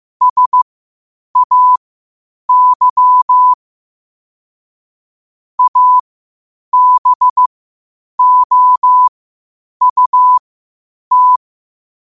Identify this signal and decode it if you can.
Morse code, heard as a keyed tone, about 15 wpm, spaced as for 8 wpm, 1000 Hz. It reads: SAY ABOUT
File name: qotdfiles / 8wpm